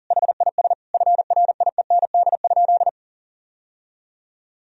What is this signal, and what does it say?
Morse code, heard as a keyed tone, about 40 wpm, 700 Hz. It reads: HIS FRIEND?
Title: Sentences from Top 1000 Words - Review Part 3 - 40wpm